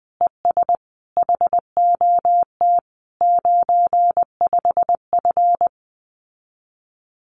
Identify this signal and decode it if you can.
Morse code, heard as a keyed tone, about 20 wpm, 700 Hz. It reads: ES HOT 95F
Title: Common QSO Phrases Encore 20wpm